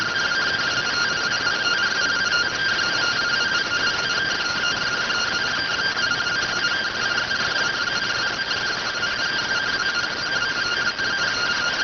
Two-tone teleprinter signal, about 170 Hz shift.